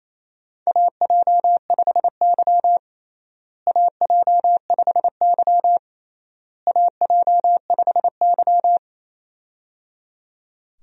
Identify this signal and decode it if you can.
Morse code, heard as a keyed tone, about 28 wpm, 700 Hz. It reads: AJ5Y AJ5Y AJ5Y
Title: USA Call Signs - 4 Characters - Repeated 3x - 28wpm